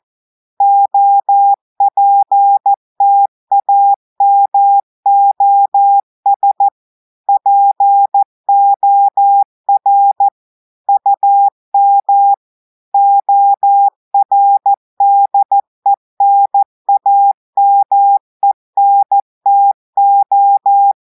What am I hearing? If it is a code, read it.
Morse code, heard as a keyed tone, about 14 wpm, 800 Hz. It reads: OPTAMOS POR UM ORDENAMENTO
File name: audios / portugais / portugais texte en clair 1 vitesse 14 mots minutes